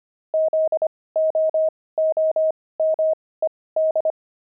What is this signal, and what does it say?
Morse code, heard as a keyed tone, about 25 wpm, spaced as for 18 wpm, 630 Hz. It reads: ZOOMED
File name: pangram-fs-18-cs-25